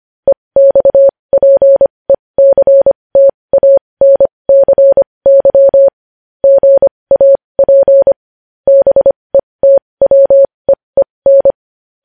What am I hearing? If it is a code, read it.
Morse code, heard as a keyed tone, about 25 wpm, spaced as for 20 wpm, 550 Hz. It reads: EXPECTANCY GAP BETWEEN
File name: podcast / News headlines-1x-25x20